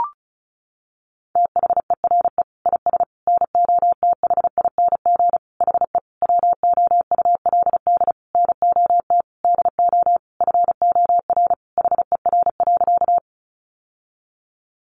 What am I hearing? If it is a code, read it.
Morse code, heard as a keyed tone, about 34 wpm, 700 Hz. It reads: THERE IS NOTHING HE WOULD NOT DO FOR HER.
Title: Sentences from Top 600 Words 35wpm